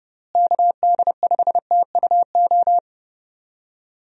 Morse code, heard as a keyed tone, about 30 wpm, 700 Hz. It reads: KD5TUO
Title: USA Call Signs - 6 Characters - Repeated 2x - 30wpm